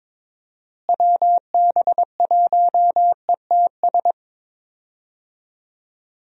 Morse code, heard as a keyed tone, about 22 wpm, 700 Hz. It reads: WB1ETS
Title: USA Call Signs - 6 Characters - Repeated 2x - 22wpm